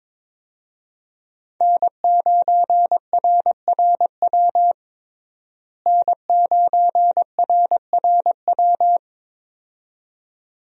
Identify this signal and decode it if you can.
Morse code, heard as a keyed tone, about 22 wpm, 700 Hz. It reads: N9RRW N9RRW
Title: USA Call Signs - 5 Characters - Repeated 2x - 22wpm